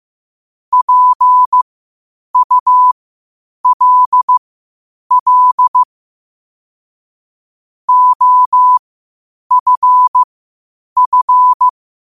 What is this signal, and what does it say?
Morse code, heard as a keyed tone, about 15 wpm, spaced as for 8 wpm, 1000 Hz. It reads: PULL OFF
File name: qotdfiles / 8wpm